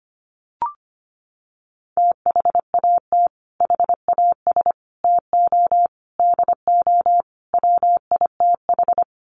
Morse code, heard as a keyed tone, about 25 wpm, 700 Hz. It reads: THAT HAS TO DO WITH
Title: Sets of 5 Words Encore2 25wpm